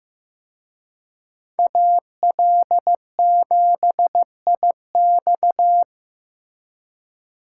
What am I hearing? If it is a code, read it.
Morse code, heard as a keyed tone, about 18 wpm, 700 Hz. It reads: AL7IX
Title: Speed Racing - USA Call Signs 15wpm